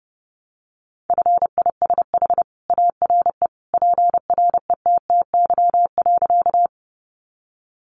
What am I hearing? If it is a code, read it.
Morse code, heard as a keyed tone, about 30 wpm, 700 Hz. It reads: FISH ARE PRETTY.
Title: Additional Easy Phrases 30wpm